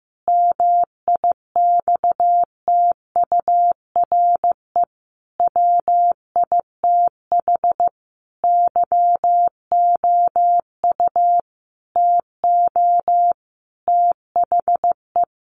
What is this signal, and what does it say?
Morse code, heard as a keyed tone, about 15 wpm, 700 Hz. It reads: MIXTURE WITH YOU TO THE
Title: Sentences from Top 1500 Words 15wpm